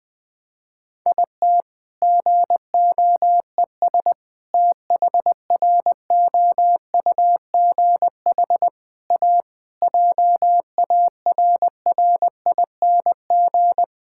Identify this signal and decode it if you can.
Morse code, heard as a keyed tone, about 21 wpm, 700 Hz. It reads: IT GOES THROUGH A JARRING
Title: All-Age Appropriate Jokes - 20wpm